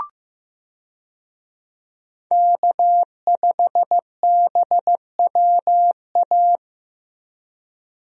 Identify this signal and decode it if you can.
Morse code, heard as a keyed tone, about 15 wpm, 700 Hz. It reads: K5BWA
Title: USA Call Signs - 5 Characters - Repeated 3x - 15wpm